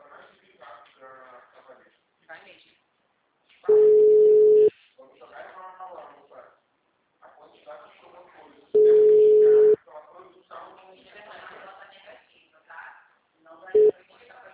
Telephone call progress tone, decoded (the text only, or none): ringback tone